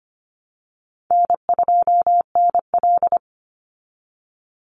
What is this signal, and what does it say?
Morse code, heard as a keyed tone, about 25 wpm, 700 Hz. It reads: N2NL
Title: Easy Call Signs Repeated 3x - 25wpm